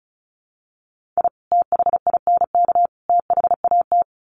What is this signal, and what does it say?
Morse code, heard as a keyed tone, about 34 wpm, 700 Hz. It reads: I THINK THAT
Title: Sets of 3 Words 35wpm